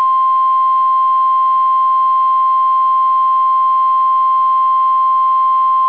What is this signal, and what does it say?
Emergency Alert System (SAME) tones (the 1050 Hz attention tone). It is heard from 0.4 to 5.5 s.